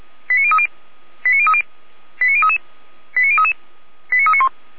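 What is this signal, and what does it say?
Five-tone selective call (ZVEI), heard as quick runs of five tones, tones of about 70 ms. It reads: 89030 89030 89033 89033 89391
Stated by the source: ZVEI five-tone call received on an SDR, from feuerwehr-orlamuende.org (Orlamünde fire brigade).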